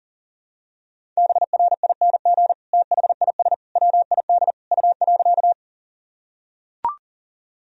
Morse code, heard as a keyed tone, about 40 wpm, 700 Hz. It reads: BRING THIS WID U.
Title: Sentences from Top 500 Words and Abbreviations 40wpm